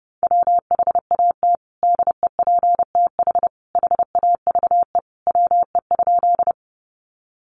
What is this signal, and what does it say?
Morse code, heard as a keyed tone, about 30 wpm, 700 Hz. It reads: WHAT DEPTH HAVE WE?
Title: Sentences from Top 1200 Words 30wpm